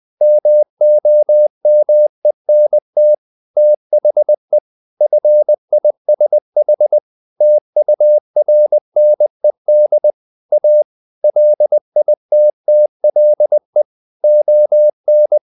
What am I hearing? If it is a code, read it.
Morse code, heard as a keyed tone, about 20 wpm, 600 Hz. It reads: MOMENT THE FISH TURNED A LITTLE ON
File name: the old man and the sea part 2 20wpm